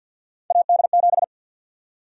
Morse code, heard as a keyed tone, about 52 wpm, 700 Hz. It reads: AD7
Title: USA Call Sign Prefixes - 50wpm